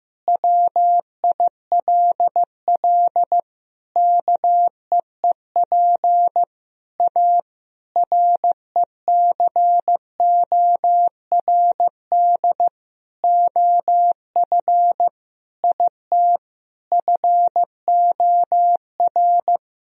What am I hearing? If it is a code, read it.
Morse code, heard as a keyed tone, about 15 wpm, 700 Hz. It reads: WILL KEEP A RECORD OF IT FOR
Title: Sentences from Top 1000 Words - Review Part 3 - 15wpm